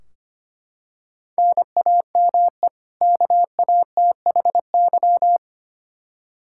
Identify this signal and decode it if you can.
Morse code, heard as a keyed tone, about 25 wpm, 700 Hz. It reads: NAME KATHY
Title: Practice QSO - Exchange 1 - Beginning - 25wpm